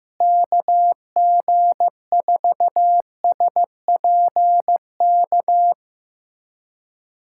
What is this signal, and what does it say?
Morse code, heard as a keyed tone, about 15 wpm, 700 Hz. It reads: KG4SPK
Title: USA Callsign DE Callsign 15wpm